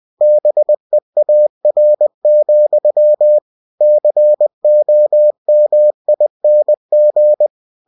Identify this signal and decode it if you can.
Morse code, heard as a keyed tone, about 20 wpm, 600 Hz. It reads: BEAR, COMING
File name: cw-books / winnie-the-pooh / winnie-20wpm-page-0000